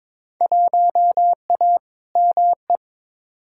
Morse code, heard as a keyed tone, about 22 wpm, 700 Hz. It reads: 1A ME
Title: Field Day Exchanges Encore - 22wpm